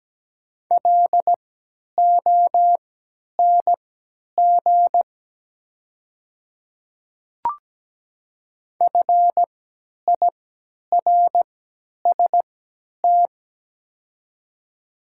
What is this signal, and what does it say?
Morse code, heard as a keyed tone, about 17 wpm, spaced as for 10 wpm, 700 Hz. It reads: LONG FIRST
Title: 3x Character Spacing - Top 100 Words - 17wpm